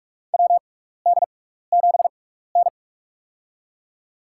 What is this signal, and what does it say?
Morse code, heard as a keyed tone, about 46 wpm, spaced as for 15 wpm, 700 Hz. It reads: WD7N
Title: USA Call Signs - 6x Character Spacing - 45wpm